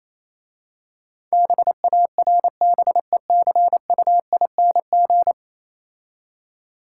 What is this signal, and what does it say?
Morse code, heard as a keyed tone, about 27 wpm, 690 Hz. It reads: BARBECUING
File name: Easy Words - Speed-Racing with Hints - 22wpm